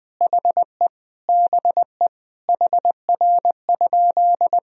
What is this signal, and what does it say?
Morse code, heard as a keyed tone, about 20 wpm, 700 Hz. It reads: HE BE HR?
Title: Sentences from Top 400 Words and Abbreviations 20wpm